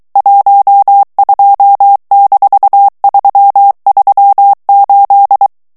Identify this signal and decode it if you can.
Morse code, heard as a keyed tone, about 23 wpm, 800 Hz. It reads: 12-338
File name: ANGLAIS texte clair 3 vitesse 22 mots minute